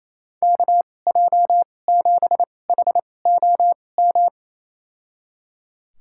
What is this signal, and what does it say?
Morse code, heard as a keyed tone, about 27 wpm, spaced as for 20 wpm, 700 Hz. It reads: KJ7HOM
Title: USA Call Signs - 2x Character Spacing - 28wpm